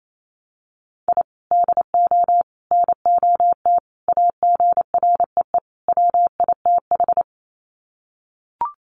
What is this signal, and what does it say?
Morse code, heard as a keyed tone, about 28 wpm, 700 Hz. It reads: I DO NOT AGREE WITH
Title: Sets of 5 Words Encore 28wpm